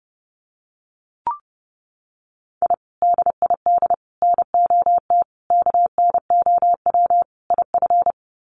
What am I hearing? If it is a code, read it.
Morse code, heard as a keyed tone, about 30 wpm, 700 Hz. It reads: I DID NOT KNOW IF
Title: Sets of 5 Words Encore 30wpm